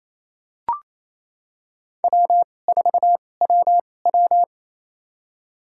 Morse code, heard as a keyed tone, about 28 wpm, spaced as for 20 wpm, 700 Hz. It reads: W4WW
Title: USA Call Signs - 4 Characters - 2x Character Spacing - 28wpm